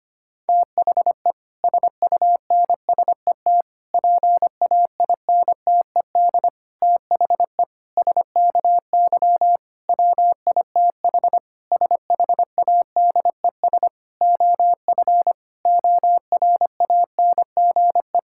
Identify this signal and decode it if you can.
Morse code, heard as a keyed tone, about 25 wpm, 700 Hz. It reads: THE SUNSET PAINTED THE SKY WITH SHADES OF ORANGE
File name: practice / 25wpm / Difficult Sentences with Hints - 25wpm